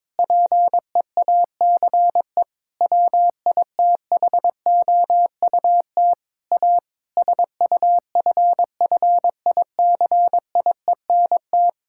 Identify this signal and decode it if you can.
Morse code, heard as a keyed tone, about 22 wpm, 700 Hz. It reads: PEACE WITHOUT A SUFFICIENT
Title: Sentences from Top 1600 Words 22wpm